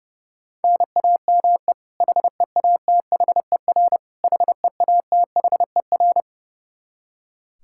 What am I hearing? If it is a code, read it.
Morse code, heard as a keyed tone, about 32 wpm, 700 Hz. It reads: NAME HEATHER HEATHER
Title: K1USN SST QSO Practice 30wpm